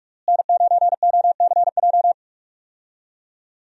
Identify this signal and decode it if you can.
Morse code, heard as a keyed tone, about 46 wpm, 700 Hz. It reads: N9OCJ
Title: USA Call Signs - 5 Characters - Repeated 3x - 45wpm